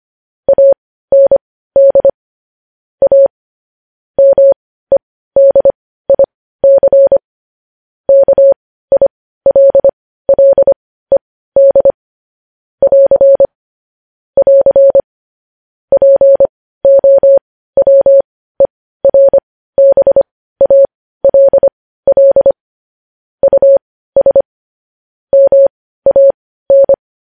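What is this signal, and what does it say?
Morse code, heard as a keyed tone, about 25 wpm, spaced as for 15 wpm, 550 Hz. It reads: AND A MEDIC KILLED <AR> <AR> POWERBALL US MAN
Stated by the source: Morse code news headlines sent at 15 wpm.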